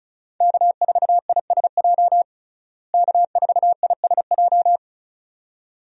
Morse code, heard as a keyed tone, about 34 wpm, 700 Hz. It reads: K4ISJ K4ISJ
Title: USA Call Signs - 5 Characters - Repeated 2x - 35wpm